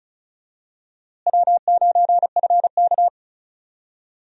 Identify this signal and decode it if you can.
Morse code, heard as a keyed tone, about 36 wpm, 700 Hz. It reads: W9FK
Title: USA Call Signs - 4 Characters - Repeated 2x - 35wpm